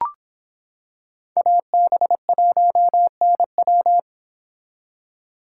Morse code, heard as a keyed tone, about 26 wpm, 700 Hz. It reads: AB1NW